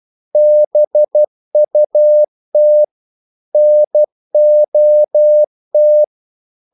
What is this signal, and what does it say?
Morse code, heard as a keyed tone, about 12 wpm, 600 Hz. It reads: BUT NOT
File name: CW / Practice / EnglishQuotations0112